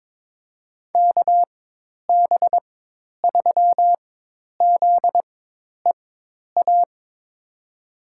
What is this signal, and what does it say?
Morse code, heard as a keyed tone, about 21 wpm, spaced as for 10 wpm, 700 Hz. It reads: KB3ZEA
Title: USA Call Signs - 6 Characters - 4x Character Spacing - 22wpm